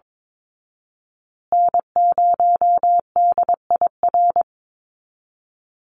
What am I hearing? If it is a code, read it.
Morse code, heard as a keyed tone, about 22 wpm, 700 Hz. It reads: N0DIR
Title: Call Signs Repeated 2x - 22wpm